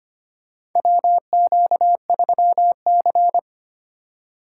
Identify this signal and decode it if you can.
Morse code, heard as a keyed tone, about 25 wpm, 700 Hz. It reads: WQ3C
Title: USA Call Signs - 4 Characters - 2x Character Spacing - 25wpm